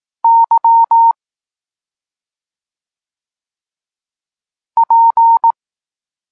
Morse code, heard as a keyed tone, about 18 wpm, 920 Hz. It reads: Y P
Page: Tekst 32 – Doskonalenie odbioru poznanych liter